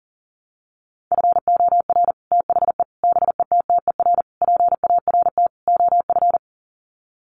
Morse code, heard as a keyed tone, about 40 wpm, 700 Hz. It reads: FOR THE BETTER PART OF
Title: Sets of 5 Words Encore2 40wpm